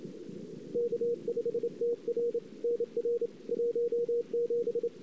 Morse code, heard as a keyed tone, about 29 wpm, 470 Hz. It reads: K5TR NR 17
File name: CW / 2008-ss-cw-k5tr-op-n5rz -23:00